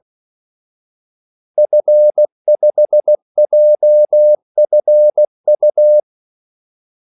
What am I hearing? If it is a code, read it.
Morse code, heard as a keyed tone, about 16 wpm, 600 Hz. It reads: F5JFU
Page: Kaldesignaler 16 wpm